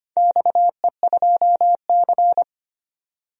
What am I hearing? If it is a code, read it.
Morse code, heard as a keyed tone, about 25 wpm, 700 Hz. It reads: XE2C
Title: DXCC-100 Call Signs - 4 Characters - 25wpm